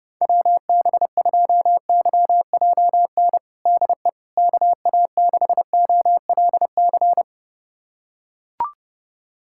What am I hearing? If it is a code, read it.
Morse code, heard as a keyed tone, about 32 wpm, 700 Hz. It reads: WB2YJN DE KA6OLC
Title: USA Callsign DE Callsign - 6 Characters - 30wpm